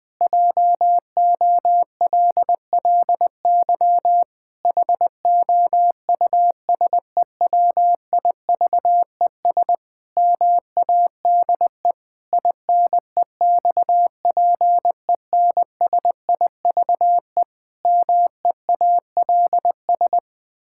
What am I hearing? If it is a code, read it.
Morse code, heard as a keyed tone, about 21 wpm, 700 Hz. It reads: JOLLY HOUSEWIVES MADE INEXPENSIVE MEALS
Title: Pangram Challenge v2 - 20wpm